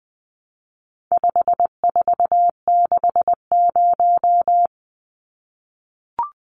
Morse code, heard as a keyed tone, about 20 wpm, 700 Hz. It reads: 5460
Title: Numbers - 4-Digit 20wpm